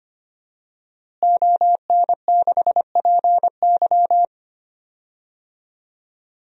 Morse code, heard as a keyed tone, about 25 wpm, 700 Hz. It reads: ON6PY
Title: DXCC-100 Call Signs - 5 Characters - Repeated 2x - 25wpm